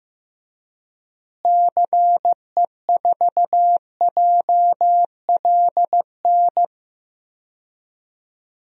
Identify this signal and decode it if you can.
Morse code, heard as a keyed tone, about 15 wpm, 700 Hz. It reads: CE4JLN
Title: DXCC-100 Call Signs - 6 Characters - Repeated 3x - 15wpm